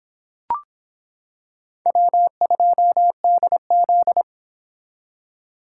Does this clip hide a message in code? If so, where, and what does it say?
Yes: Morse code from 1.9 to 4.2 s, heard as a keyed tone, about 26 wpm, 700 Hz. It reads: W2DZ